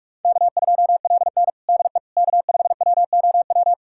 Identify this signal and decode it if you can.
Morse code, heard as a keyed tone, about 46 wpm, 700 Hz. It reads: K2LN DE K5WOW
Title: USA Callsign DE Callsign 45wpm